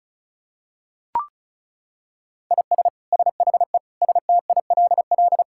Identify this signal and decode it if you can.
Morse code, heard as a keyed tone, about 36 wpm, 700 Hz. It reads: IS SHE STILL
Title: Sentences from Top 1000 Words - Review Part 6 - 35wpm